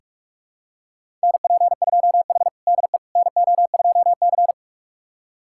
Morse code, heard as a keyed tone, about 44 wpm, 700 Hz. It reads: NP2H DE NO2C
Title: USA Callsign DE Callsign - 4 Characters - 45wpm